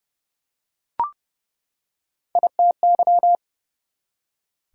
Morse code, heard as a keyed tone, about 30 wpm, 700 Hz. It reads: ITY
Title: Prefixes and Suffixes 30wpm